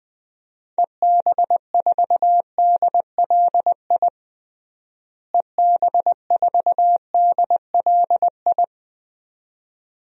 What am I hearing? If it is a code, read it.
Morse code, heard as a keyed tone, about 20 wpm, 700 Hz. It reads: EB4DLI EB4DLI
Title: DXCC-100 Call Signs - 6 Characters - Repeated 2x - 20wpm